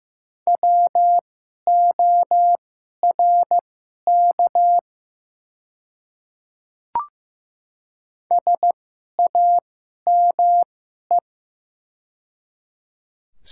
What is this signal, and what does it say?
Morse code, heard as a keyed tone, about 15 wpm, spaced as for 11 wpm, 700 Hz. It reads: WORK SAME
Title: 2x Character Spacing - Top 200 Words - 15wpm